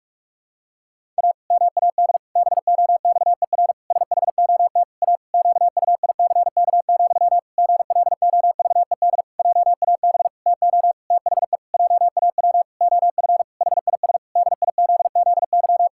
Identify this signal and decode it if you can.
Morse code, heard as a keyed tone, about 44 wpm, 700 Hz. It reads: A MAD BOXER SHOT A QUICK, GLOVED JAB TO THE JAW OF HIS DIZZY
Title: Pangram Challenge v2 - 45wpm